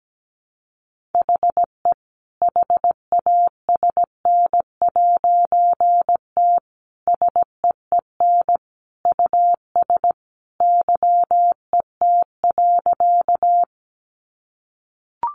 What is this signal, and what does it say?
Morse code, heard as a keyed tone, about 17 wpm, 700 Hz. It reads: HE HASN'T SEEN US YET.
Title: A Sound Of Thunder 17wpm